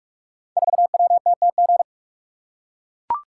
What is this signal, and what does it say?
Morse code, heard as a keyed tone, about 44 wpm, 700 Hz. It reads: 4WTTG
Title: Code Groups - 5 Letter:Numbers - 45wpm